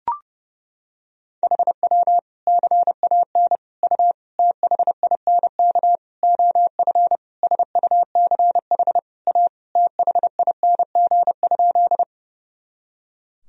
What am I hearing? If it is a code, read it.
Morse code, heard as a keyed tone, about 30 wpm, 700 Hz. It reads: HW CAN U THINK OF SUCH A THING?
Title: Sentences from Top 200 Words and Abbreviations 30wpm